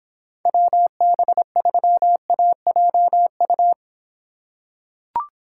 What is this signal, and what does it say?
Morse code, heard as a keyed tone, about 24 wpm, 700 Hz. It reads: WB3AJU